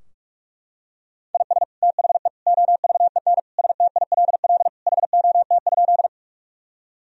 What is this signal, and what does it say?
Morse code, heard as a keyed tone, about 41 wpm, 700 Hz. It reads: IS THE OVEN STILL HOT?
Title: Sentences from Top 1000 Words - Review Part 4 - 45wpm